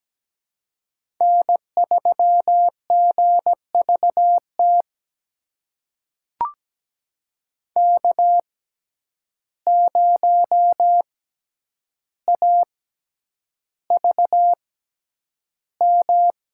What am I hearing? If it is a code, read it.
Morse code, heard as a keyed tone, about 17 wpm, spaced as for 9 wpm, 700 Hz. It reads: N3GVT K0AVM
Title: USA Call Signs - 5 Characters - 6x Character Spacing - 17wpm